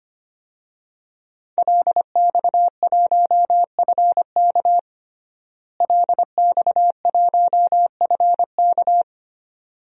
Morse code, heard as a keyed tone, about 25 wpm, 700 Hz. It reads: LX1FK LX1FK
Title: DXCC-100 Call Signs - 5 Characters - Repeated 2x - 25wpm